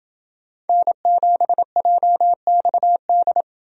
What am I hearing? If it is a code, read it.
Morse code, heard as a keyed tone, about 29 wpm, 700 Hz. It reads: N7JXD